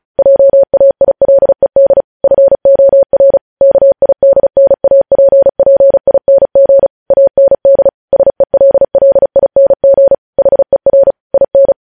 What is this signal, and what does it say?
Morse code, heard as a keyed tone, about 36 wpm, 550 Hz. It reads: JAILED FOR KIDNAPPING AND SELLING HER IN
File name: podcast / News headlines-1x-35x35